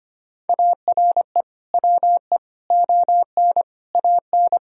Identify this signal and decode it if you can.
Morse code, heard as a keyed tone, about 25 wpm, 700 Hz. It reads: ARE WE ON AN
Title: Sentences from Top 1000 Words - Review Part 3 - 25wpm